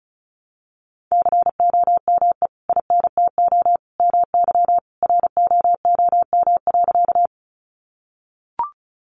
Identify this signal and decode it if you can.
Morse code, heard as a keyed tone, about 34 wpm, 700 Hz. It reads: COME INTO MY ROOM.
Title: Sentences from Top 600 Words 35wpm